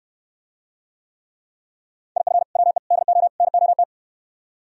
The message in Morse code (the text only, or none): IF WE DO NOT